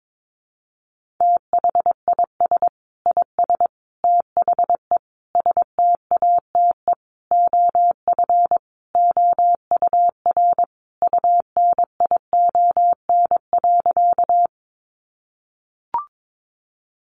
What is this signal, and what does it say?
Morse code, heard as a keyed tone, about 22 wpm, 700 Hz. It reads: THIS IS THE STATE OF OUR UNION.
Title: Sentences from Top 1400 Words 22wpm